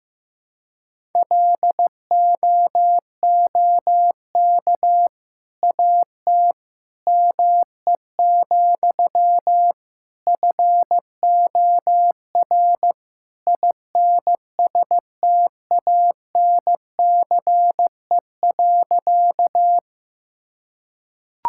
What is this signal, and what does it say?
Morse code, heard as a keyed tone, about 15 wpm, 700 Hz. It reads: LOOK AT ME, FOR INSTANCE.
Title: Sentences from Top 1000 Words - Review Part 5 - 15wpm